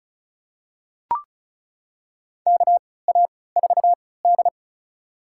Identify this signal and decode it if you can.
Morse code, heard as a keyed tone, about 36 wpm, spaced as for 20 wpm, 700 Hz. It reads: KA4D
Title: USA Call Signs - 4 Characters - 3x Character Spacing - 35wpm